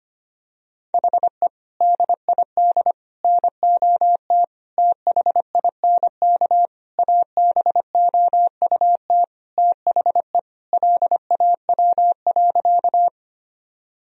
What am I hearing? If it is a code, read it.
Morse code, heard as a keyed tone, about 25 wpm, 700 Hz. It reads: HE DID NOT THINK ABOUT THE LAW.
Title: Sentences from Top 1000 Words - Review Part 3 - 25wpm